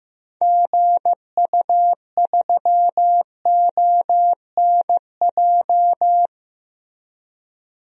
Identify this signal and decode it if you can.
Morse code, heard as a keyed tone, about 16 wpm, 700 Hz. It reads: GU3ONJ
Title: DXCC-100 Call Signs - 6 Characters - Repeated 2x - 15wpm